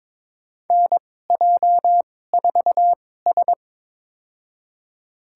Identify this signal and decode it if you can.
Morse code, heard as a keyed tone, about 22 wpm, spaced as for 16 wpm, 700 Hz. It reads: NJ4S
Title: USA Call Signs - 2x Character Spacing - 22wpm